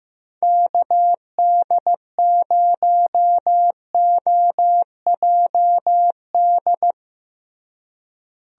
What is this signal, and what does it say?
Morse code, heard as a keyed tone, about 15 wpm, 700 Hz. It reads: KD0OJD
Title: USA Call Signs - 6 Characters - Repeated 2x - 15wpm